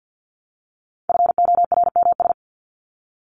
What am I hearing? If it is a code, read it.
Morse code, heard as a keyed tone, about 50 wpm, 700 Hz. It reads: FORMS
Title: ES:S Suffix - Top 100 Words with Base Words 50wpm